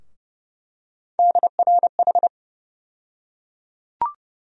Morse code, heard as a keyed tone, about 30 wpm, 700 Hz. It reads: DRH
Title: USA Call Sign Postfixes - 3 Letters 30wpm